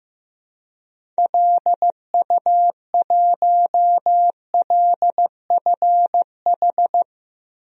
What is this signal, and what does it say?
Morse code, heard as a keyed tone, about 15 wpm, 700 Hz. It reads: LU1LFH
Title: DXCC-100 Call Signs - 6 Characters - Repeated 3x - 15wpm